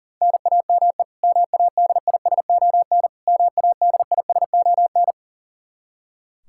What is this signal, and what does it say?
Morse code, heard as a keyed tone, about 37 wpm, 700 Hz. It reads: NAME MADISON MADISON
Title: K1USN SST QSO Practice 40wpm